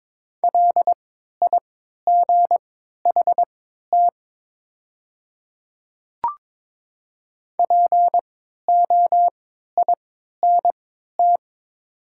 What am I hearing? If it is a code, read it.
Morse code, heard as a keyed tone, about 22 wpm, spaced as for 12 wpm, 700 Hz. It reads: LIGHT POINT
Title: 3x Character Spacing - Top 300 Words - 22wpm